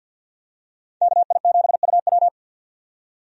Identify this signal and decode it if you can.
Morse code, heard as a keyed tone, about 47 wpm, 700 Hz. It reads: KI7UW
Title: USA Call Signs - 5 Characters - 50wpm